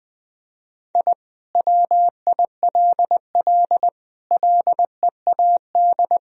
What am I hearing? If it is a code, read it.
Morse code, heard as a keyed tone, about 20 wpm, 700 Hz. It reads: I WILL LEAD
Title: Sentences from Top 1000 Words - Review Part 2 - 20wpm